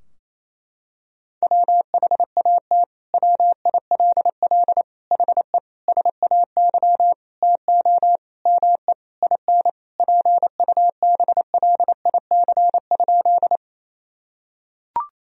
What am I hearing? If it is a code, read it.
Morse code, heard as a keyed tone, about 28 wpm, 700 Hz. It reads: WHAT WILL HE SAY TO ME IN PUBLIC?
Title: Sentences from Top 1000 Words - Review Part 3 - 28wpm